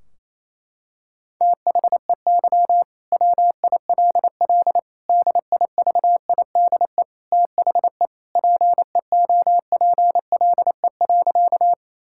Morse code, heard as a keyed tone, about 28 wpm, 700 Hz. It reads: THEY WILL DIVIDE THE PEOPLE.
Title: Sentences from Top 1600 Words 28wpm